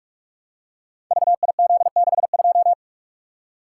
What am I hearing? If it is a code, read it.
Morse code, heard as a keyed tone, about 46 wpm, 700 Hz. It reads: VIZ62
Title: Code Groups - 5 Letter:Numbers - 45wpm